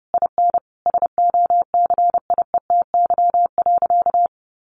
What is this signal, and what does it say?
Morse code, heard as a keyed tone, about 30 wpm, 700 Hz. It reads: IN SOCIETY.
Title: Sentences from Top 1500 Words 30wpm